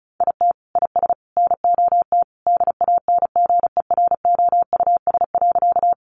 Morse code, heard as a keyed tone, about 34 wpm, 700 Hz. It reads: IT IS NOT DANGEROUS.
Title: Sentences from Top 1200 Words 35wpm